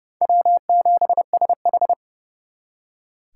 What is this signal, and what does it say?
Morse code, heard as a keyed tone, about 37 wpm, 700 Hz. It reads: W7SH
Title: Speed Racing - USA 1x2 Call Signs 30wpm